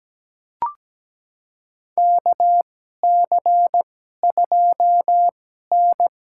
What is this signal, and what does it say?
Morse code, heard as a keyed tone, about 17 wpm, spaced as for 12 wpm, 700 Hz. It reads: KC2N
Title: USA Call Signs - 2x Character Spacing - 17wpm